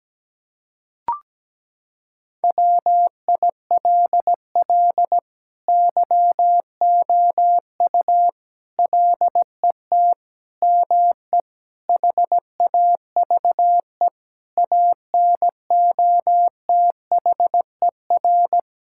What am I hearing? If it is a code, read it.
Morse code, heard as a keyed tone, about 17 wpm, 700 Hz. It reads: WILL YOU LET ME HAVE ANOTHER
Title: Sentences from Top 1600 Words 17wpm